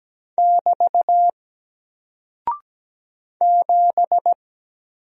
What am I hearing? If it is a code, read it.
Morse code, heard as a keyed tone, about 17 wpm, 700 Hz. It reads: <BT> 7
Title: Single Character (Letter:Number:Punctuation:Prosign) Alt 17wpm